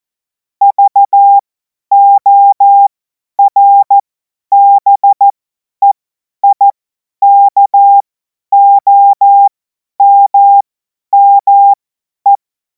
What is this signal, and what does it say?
Morse code, heard as a keyed tone, about 14 wpm, spaced as for 10 wpm, 800 Hz. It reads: VORBEIKOMME
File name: Morsekurs / Witze / Witz 17. März 2026 50 F70